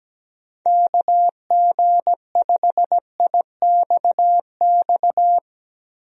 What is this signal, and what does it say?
Morse code, heard as a keyed tone, about 17 wpm, 700 Hz. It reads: KG5IXX
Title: USA Call Signs - 6 Characters - 17wpm